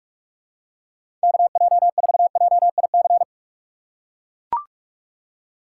Morse code, heard as a keyed tone, about 42 wpm, 700 Hz. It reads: KJ4JIC
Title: USA Call Signs - 6 Characters - 3x Character Spacing - 45wpm